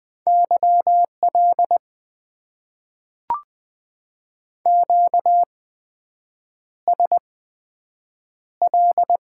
Morse code, heard as a keyed tone, about 20 wpm, spaced as for 7 wpm, 700 Hz. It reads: YL QSL
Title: Top 90 QSO Elements - 8x Character Spacing - 20wpm